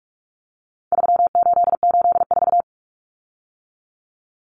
Morse code, heard as a keyed tone, about 44 wpm, 700 Hz. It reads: 3884
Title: Common Cut Numbers - 4-Digit 45wpm